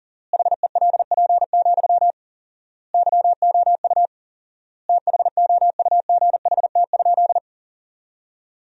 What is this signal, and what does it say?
Morse code, heard as a keyed tone, about 40 wpm, 700 Hz. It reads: HELP, YOU THOUGHT?
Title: 4x Word-Spacing - Sentences from Top 300 Words - 40wpm